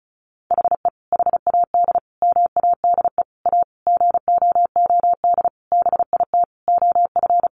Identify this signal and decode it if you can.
Morse code, heard as a keyed tone, about 34 wpm, 700 Hz. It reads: HE HAD MADE A GOOD BIT OF
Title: Sentences from Top 900 Words 35wpm